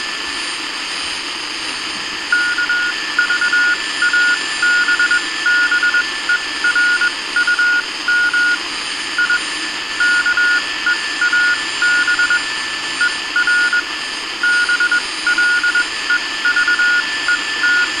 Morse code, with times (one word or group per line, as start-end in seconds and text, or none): 2.3-8.5 KVABBERUM
9.2-9.4 I
10.0-12.4 KEAB
13.0-13.8 ER
14.4-17.8 BLEVET